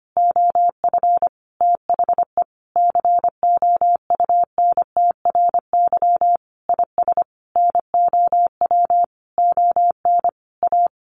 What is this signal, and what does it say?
Morse code, heard as a keyed tone, about 25 wpm, 700 Hz. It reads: OF THE COUNTRY IS NOW ON A
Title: Sentences from Top 1500 Words 25wpm